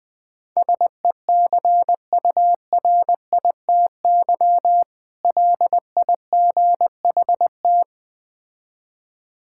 Morse code, heard as a keyed tone, about 20 wpm, 700 Hz. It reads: SECURITY LIGHT
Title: Difficult Words - Speed-Racing with Hints - 20wpm